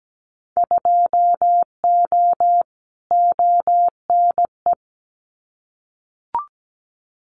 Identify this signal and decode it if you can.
Morse code, heard as a keyed tone, about 17 wpm, 700 Hz. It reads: 2O ONE
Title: Winter Field Day Exchanges Encore 17wpm